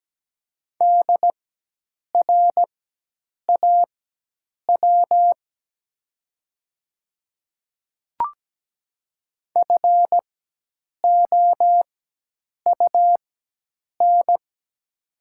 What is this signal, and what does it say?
Morse code, heard as a keyed tone, about 17 wpm, spaced as for 8 wpm, 700 Hz. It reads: DRAW FOUN
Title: 4x Character Spacing - Top 300 Words - 17wpm